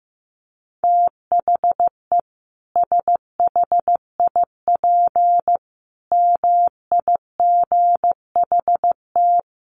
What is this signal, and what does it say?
Morse code, heard as a keyed tone, about 15 wpm, 700 Hz. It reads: THE SHIP MIGHT
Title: Sentences from Top 1400 Words 15wpm